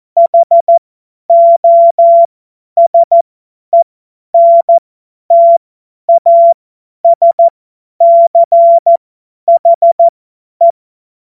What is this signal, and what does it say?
Morse code, heard as a keyed tone, about 14 wpm, spaced as for 10 wpm, 680 Hz. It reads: HOSENTASCHE
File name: Morsekurs / Witze / Witz 03. August 2021, 50 F70